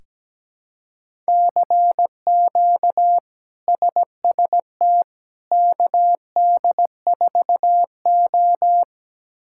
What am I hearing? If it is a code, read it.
Morse code, heard as a keyed tone, about 17 wpm, 700 Hz. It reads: CQ SST KD4O
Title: K1USN SST QSO Practice - 17wpm